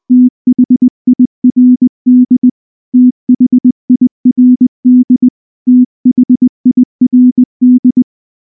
Morse code, heard as a keyed tone, about 19 wpm, 250 Hz. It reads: THIRD THIRD THIRD
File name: morse / morse-training-20WPM-1